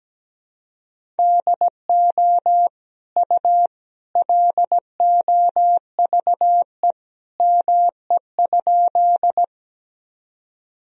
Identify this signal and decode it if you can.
Morse code, heard as a keyed tone, about 17 wpm, 700 Hz. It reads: DO U LOVE ME?
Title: Sentences from Top 400 Words and Abbreviations 17wpm